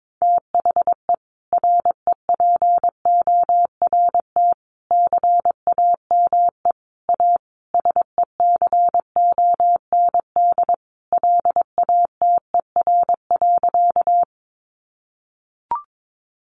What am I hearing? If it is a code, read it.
Morse code, heard as a keyed tone, about 22 wpm, 700 Hz. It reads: THE REPORT CAME A SECOND LATER.
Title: Sentences from Top 900 Words 22wpm